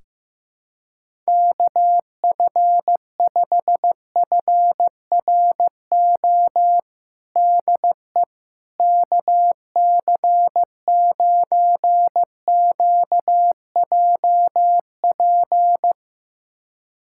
Morse code, heard as a keyed tone, about 15 wpm, 700 Hz. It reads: KF5FRO DE KC9QJP
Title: USA Callsign DE Callsign - 6 Characters - 15wpm